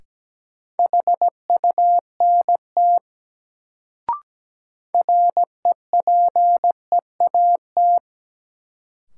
Morse code, heard as a keyed tone, about 17 wpm, 700 Hz. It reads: HUNT REPEAT
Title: Sentences from Top 1600 Words - New Word Drill - 17wpm